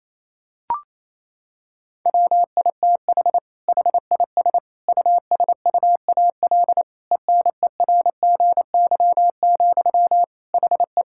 Morse code, heard as a keyed tone, about 28 wpm, 700 Hz. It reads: WITH HIS USUAL ENERGY, HE
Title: Sentences from Top 1000 Words - Review Part 6 - 28wpm